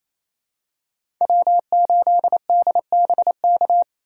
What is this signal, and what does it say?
Morse code, heard as a keyed tone, about 28 wpm, 700 Hz. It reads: W8DBK
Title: USA Call Signs - 5 Characters - 2x Character Spacing - 28wpm